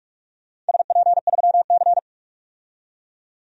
Morse code, heard as a keyed tone, about 46 wpm, 700 Hz. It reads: SP3C
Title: DXCC-100 Call Signs - 4 Characters - 45wpm